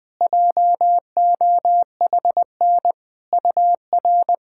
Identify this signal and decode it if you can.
Morse code, heard as a keyed tone, about 20 wpm, 700 Hz. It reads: JOHN UR
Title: SOTA QSO Practice 20wpm